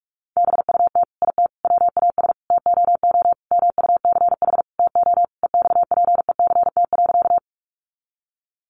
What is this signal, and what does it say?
Morse code, heard as a keyed tone, about 44 wpm, 700 Hz. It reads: BUT IT WAS TOO MUCH TO EXPECT.
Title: Sentences from Top 1100 Words 45wpm